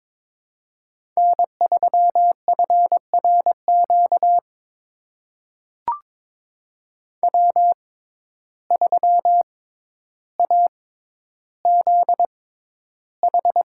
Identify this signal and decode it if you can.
Morse code, heard as a keyed tone, about 22 wpm, spaced as for 11 wpm, 700 Hz. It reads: N3FRQ W3AZH
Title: USA Call Signs - 5 Characters - 6x Character Spacing - 22wpm